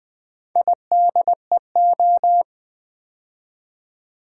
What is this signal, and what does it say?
Morse code, heard as a keyed tone, about 20 wpm, 700 Hz. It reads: IDEO
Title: Build A Word - 4 Letters At A Time - 11-Letter Words - 20wpm